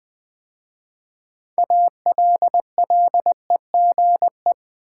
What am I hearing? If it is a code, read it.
Morse code, heard as a keyed tone, about 20 wpm, 700 Hz. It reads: ALLEGE
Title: 6 Letter Words - First 3 Letters Given - 20wpm